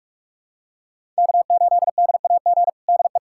Morse code, heard as a keyed tone, about 46 wpm, 700 Hz. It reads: K8DAG DE
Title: USA Callsign DE Callsign - 5 Characters - 45wpm